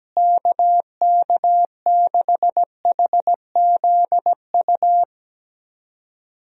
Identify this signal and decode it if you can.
Morse code, heard as a keyed tone, about 17 wpm, 700 Hz. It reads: KK6HZU
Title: USA Call Signs - 6 Characters - Repeated 3x - 17wpm